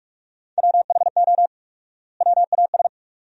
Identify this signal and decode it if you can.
Morse code, heard as a keyed tone, about 46 wpm, 700 Hz. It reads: WHO WAS
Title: 4x Word-Spacing - Sentences from Top 100 Words - 45wpm